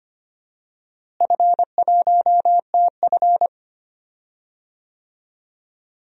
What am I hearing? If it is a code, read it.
Morse code, heard as a keyed tone, about 25 wpm, 700 Hz. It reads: F1TF
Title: DXCC-100 Call Signs - 4 Characters - Repeated 2x - 25wpm